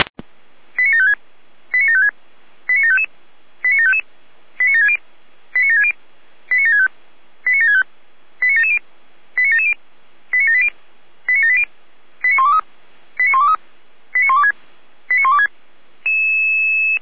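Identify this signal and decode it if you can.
Five-tone selective call (ZVEI), heard as quick runs of five tones, tones of about 70 ms. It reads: 89756 89756 89755 89755 89760 89760 89765 89765 89770 89770 89780 89780 89123 89123 89126 89126
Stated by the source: ZVEI five-tone selective call received with an SDR.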